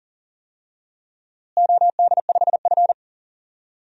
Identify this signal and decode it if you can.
Morse code, heard as a keyed tone, about 40 wpm, 700 Hz. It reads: OD5F
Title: DXCC-100 Call Signs - 4 Characters - Repeated 3x - 40wpm